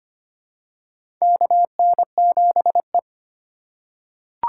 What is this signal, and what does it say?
Morse code, heard as a keyed tone, about 25 wpm, 700 Hz. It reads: KN7E
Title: USA Call Signs - 4 Characters - 3x Character Spacing - 25wpm